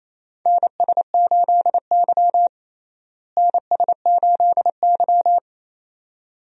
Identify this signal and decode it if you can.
Morse code, heard as a keyed tone, about 28 wpm, 700 Hz. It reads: NS8Y NS8Y
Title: USA Call Signs - 4 Characters - Repeated 2x - 28wpm